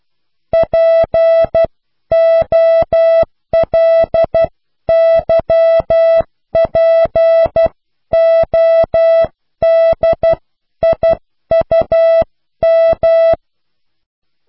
Morse code, heard as a keyed tone, about 12 wpm, 660 Hz. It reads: POLYPODIUM
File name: cw / pseudo worden(10)